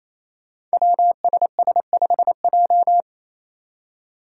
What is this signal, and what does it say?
Morse code, heard as a keyed tone, about 28 wpm, 700 Hz. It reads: WSS5J
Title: Five Challenging Letter-Number Combinations 28wpm